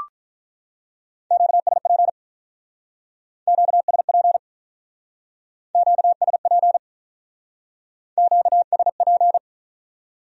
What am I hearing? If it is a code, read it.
Morse code, heard as a keyed tone, about 44 wpm, 700 Hz. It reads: QSP QSP QSP QSP